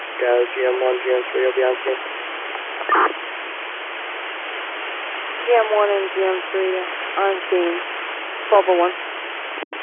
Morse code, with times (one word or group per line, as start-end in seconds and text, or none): none